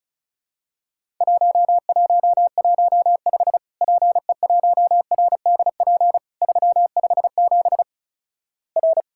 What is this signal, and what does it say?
Morse code, heard as a keyed tone, about 36 wpm, 700 Hz. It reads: 1115 PE1RDP 357 R
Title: Worked All Europe DX Contest - QTC Exchange - 35wpm